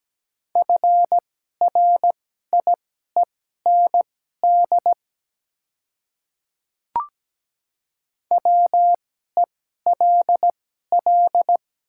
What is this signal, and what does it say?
Morse code, heard as a keyed tone, about 17 wpm, spaced as for 12 wpm, 700 Hz. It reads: FRIEND WELL
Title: 2x Character Spacing - Top 300 Words - 17wpm